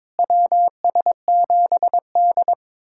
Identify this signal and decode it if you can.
Morse code, heard as a keyed tone, about 22 wpm, 700 Hz. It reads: WS7D
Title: USA Callsign DE Callsign - 4 Characters - 22wpm